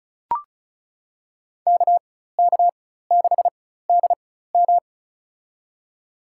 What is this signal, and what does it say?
Morse code, heard as a keyed tone, about 34 wpm, spaced as for 16 wpm, 700 Hz. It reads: KK6DM
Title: USA Call Signs - 5 Characters - 4x Character Spacing - 35wpm